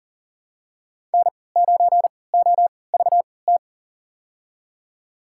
Morse code, heard as a keyed tone, about 38 wpm, spaced as for 22 wpm, 700 Hz. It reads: N9OVT
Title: USA Call Signs - 5 Characters - 3x Character Spacing - 40wpm